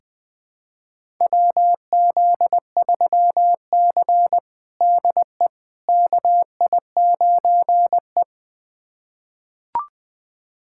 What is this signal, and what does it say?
Morse code, heard as a keyed tone, about 20 wpm, 700 Hz. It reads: WZ3C DE KI9E
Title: USA Callsign DE Callsign - 4 Characters - 20wpm